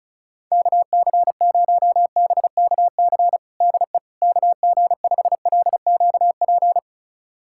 Morse code, heard as a keyed tone, about 36 wpm, 700 Hz. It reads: KC0BKC DE KG5LQP
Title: USA Callsign DE Callsign - 6 Characters - 35wpm